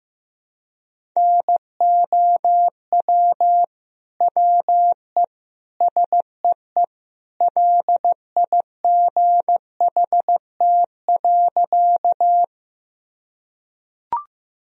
Morse code, heard as a keyed tone, about 16 wpm, 700 Hz. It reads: NOW WE SEE LIGHT.
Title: Sentences from Top 1000 Words - Review Part 1 - 15wpm